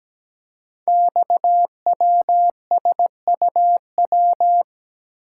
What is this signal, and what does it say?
Morse code, heard as a keyed tone, about 17 wpm, 700 Hz. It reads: XWSUW
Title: Code Groups - 5 Letters - 17wpm